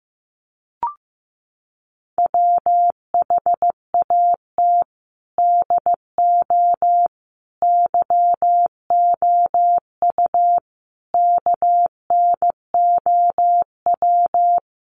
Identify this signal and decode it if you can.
Morse code, heard as a keyed tone, about 15 wpm, 700 Hz. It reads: WHAT DO YOU KNOW
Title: Sets of 5 Words Encore 15wpm